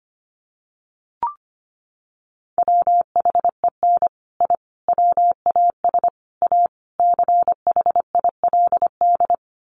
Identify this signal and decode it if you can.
Morse code, heard as a keyed tone, about 25 wpm, 700 Hz. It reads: WHEN I WAS A CHILD
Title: Sets of 5 Words Encore 25wpm